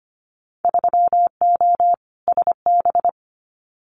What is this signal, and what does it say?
Morse code, heard as a keyed tone, about 25 wpm, 700 Hz. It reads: 3O SB
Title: Winter Field Day Exchanges Encore 25wpm